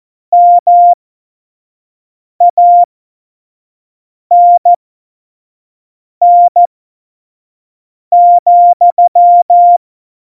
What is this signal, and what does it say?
Morse code, heard as a keyed tone, about 14 wpm, spaced as for 5 wpm, 700 Hz. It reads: MANN,
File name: Morsekurs / Witze / Witz 27 Juni 17, Tempo 25 F70